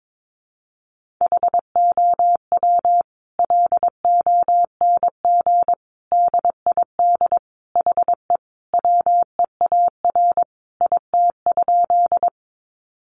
Morse code, heard as a keyed tone, about 22 wpm, 700 Hz. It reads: HOW LONG DID HE WEAR IT?
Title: Sentences from Top 1500 Words 22wpm